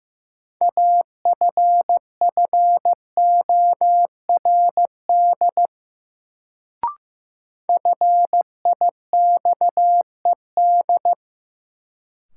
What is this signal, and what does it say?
Morse code, heard as a keyed tone, about 15 wpm, 700 Hz. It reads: AFFORD FIXED
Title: Sentences from Top 1100 Words - New Word Drill - 15wpm